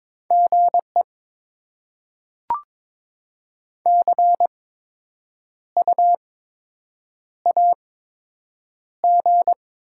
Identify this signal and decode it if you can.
Morse code, heard as a keyed tone, about 22 wpm, 700 Hz. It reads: G E C U A G
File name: Top 20 QSO Elements - 8x Character Spacing - 22wpm